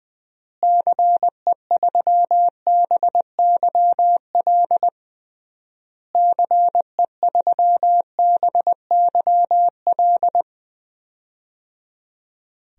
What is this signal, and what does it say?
Morse code, heard as a keyed tone, about 20 wpm, 700 Hz. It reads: CE3BYL CE3BYL
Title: DXCC-100 Call Signs - 6 Characters - Repeated 3x - 20wpm